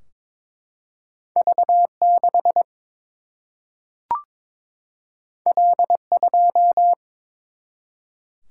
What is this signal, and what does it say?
Morse code, heard as a keyed tone, about 22 wpm, 700 Hz. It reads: V 6 L 2
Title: Single-Letter Single-Number 22wpm